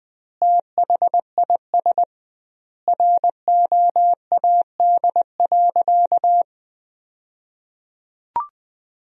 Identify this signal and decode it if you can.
Morse code, heard as a keyed tone, about 20 wpm, 700 Hz. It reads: THIS ROAD.
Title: 2x Word-Spacing - Sentences from Top 400 Words - 20wpm